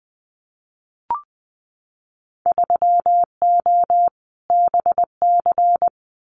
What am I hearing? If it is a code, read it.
Morse code, heard as a keyed tone, about 20 wpm, 700 Hz. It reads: 3O BC
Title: Winter Field Day Exchanges Encore 20wpm